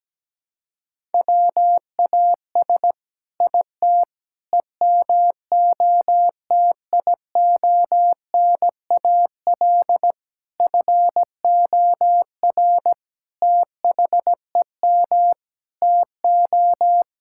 Morse code, heard as a keyed tone, about 17 wpm, 700 Hz. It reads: WAS IT EMOTIONAL FOR THEM TO
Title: Sentences from Top 1000 Words - Review Part 5 - 17wpm